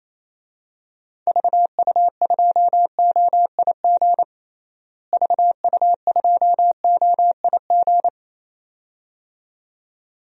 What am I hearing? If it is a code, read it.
Morse code, heard as a keyed tone, about 28 wpm, 700 Hz. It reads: VU2OIG VU2OIG
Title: DXCC-100 Call Signs - 6 Characters - Repeated 2x - 28wpm